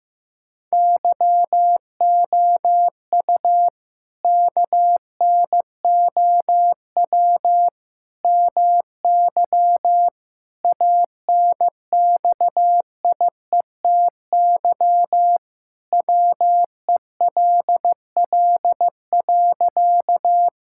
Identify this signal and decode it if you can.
Morse code, heard as a keyed tone, about 15 wpm, 700 Hz. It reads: YOU KNOW MY ANXIETY WELL.
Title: Sentences from Top 1600 Words 15wpm